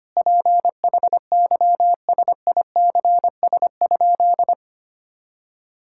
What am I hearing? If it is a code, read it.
Morse code, heard as a keyed tone, about 25 wpm, 700 Hz. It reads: PHYSICS?
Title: Sentences from Top 1000 Words - Review Part 5 - 25wpm